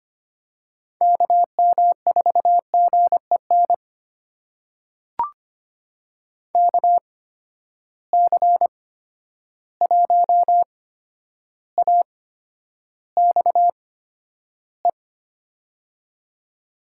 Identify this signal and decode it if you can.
Morse code, heard as a keyed tone, about 25 wpm, spaced as for 11 wpm, 700 Hz. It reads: KM4GEN KC1AXE
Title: USA Call Signs - 6 Characters - 8x Character Spacing - 25wpm